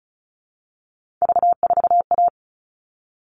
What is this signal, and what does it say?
Morse code, heard as a keyed tone, about 34 wpm, 700 Hz. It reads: V4A
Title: All Cut Numbers - 3-Digit 35wpm